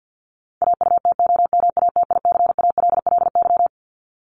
Morse code, heard as a keyed tone, about 50 wpm, 700 Hz. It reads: AUTOMATICALLY
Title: Top 1751-2000 Words 50wpm